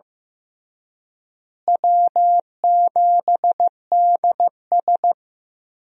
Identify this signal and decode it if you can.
Morse code, heard as a keyed tone, about 14 wpm, 700 Hz. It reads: W7DS
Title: USA Call Signs - 4 Characters - Repeated 3x - 15wpm